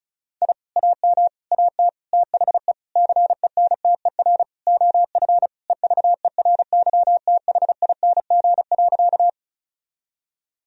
Morse code, heard as a keyed tone, about 36 wpm, 700 Hz. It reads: I AM AT THE CENTER OF EVERYTHING.
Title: Sentences from Top 1000 Words - Review Part 2 - 35wpm